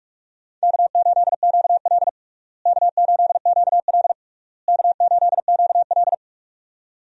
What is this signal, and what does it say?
Morse code, heard as a keyed tone, about 47 wpm, 700 Hz. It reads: K8QL K8QL K8QL
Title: USA Call Signs - 4 Characters - Repeated 3x - 45wpm